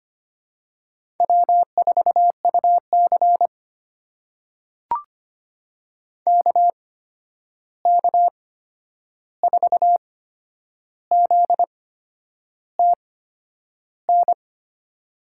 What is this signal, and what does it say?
Morse code, heard as a keyed tone, about 25 wpm, spaced as for 9 wpm, 700 Hz. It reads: W4UC KK4ZTN
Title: USA Call Signs - 8x Character Spacing - 25wpm